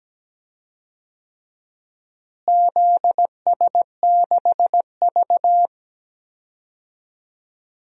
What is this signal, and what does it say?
Morse code, heard as a keyed tone, about 17 wpm, 700 Hz. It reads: ZS6V
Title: DXCC-100 Call Signs - 4 Characters - Repeated 2x - 17wpm